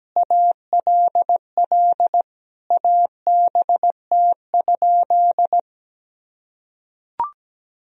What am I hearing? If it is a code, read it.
Morse code, heard as a keyed tone, about 17 wpm, 700 Hz. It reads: ALL ABT?
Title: Sentences from Top 100 Words and Abbreviations 17wpm